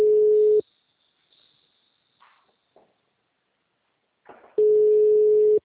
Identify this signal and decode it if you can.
Ringback tone (Europe, 425 Hz).